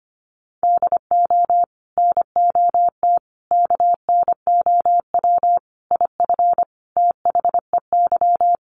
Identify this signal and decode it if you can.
Morse code, heard as a keyed tone, about 25 wpm, 700 Hz. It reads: DO NOT KNOW IF THEY
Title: Sets of 5 Words Encore 25wpm